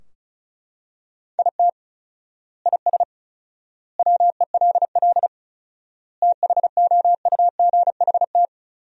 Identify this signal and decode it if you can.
Morse code, heard as a keyed tone, about 34 wpm, 700 Hz. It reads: IT IS WELL THOUGHT
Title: 4x Word-Spacing - Sentences from Top 300 Words - 35wpm